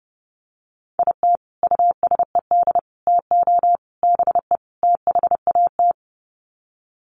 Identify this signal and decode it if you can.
Morse code, heard as a keyed tone, about 30 wpm, 700 Hz. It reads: IT USED TO BE THAT
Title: Sets of 5 Words Encore2 30wpm